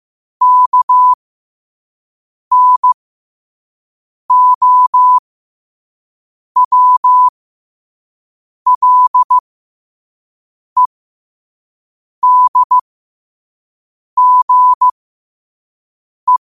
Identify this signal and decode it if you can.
Morse code, heard as a keyed tone, about 15 wpm, spaced as for 5 wpm, 1000 Hz. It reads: KNOWLEDGE
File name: qotdfiles / 5wpm